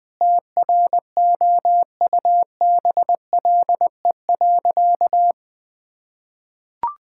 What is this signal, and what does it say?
Morse code, heard as a keyed tone, about 20 wpm, 700 Hz. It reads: TROUBLE.
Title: Sentences with 9-Word Phrases - 20wpm